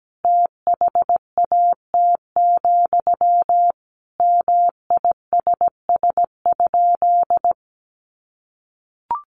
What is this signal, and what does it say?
Morse code, heard as a keyed tone, about 17 wpm, 700 Hz. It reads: THAT, MISS?
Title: Sentences from Top 500 Words 17wpm